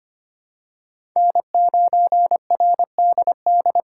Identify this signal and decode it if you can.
Morse code, heard as a keyed tone, about 25 wpm, 700 Hz. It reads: N9RDD
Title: USA Call Signs - 5 Characters - 4x Character Spacing - 25wpm